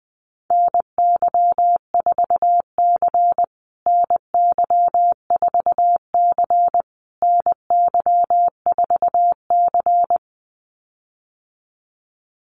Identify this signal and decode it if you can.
Morse code, heard as a keyed tone, about 20 wpm, 700 Hz. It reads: NY4C NY4C NY4C
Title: USA Call Signs Repeated 3x 20wpm